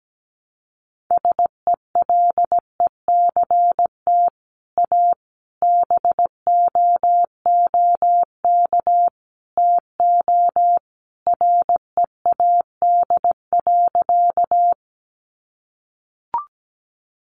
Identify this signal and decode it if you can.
Morse code, heard as a keyed tone, about 17 wpm, 700 Hz. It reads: SELECT A BOOK TO READ.
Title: Sentences from Top 1300 Words 17wpm